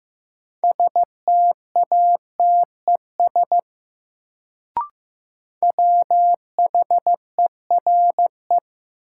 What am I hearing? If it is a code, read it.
Morse code, heard as a keyed tone, about 15 wpm, 700 Hz. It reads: STATES WHERE
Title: Sets of 2 Words - New Word Drill - 15wpm